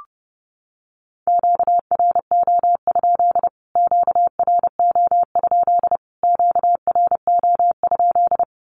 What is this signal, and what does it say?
Morse code, heard as a keyed tone, about 30 wpm, 700 Hz. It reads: QRO? QRO? QRO?
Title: Q-Codes Repeated 3x 30wpm